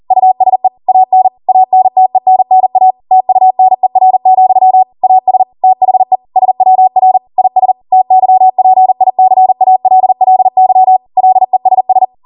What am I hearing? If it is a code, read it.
Morse code, heard as a keyed tone, about 40 wpm, 750 Hz. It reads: USE AN ANTENNA TUNER, AS THE SWR IS TYPICALLY LESS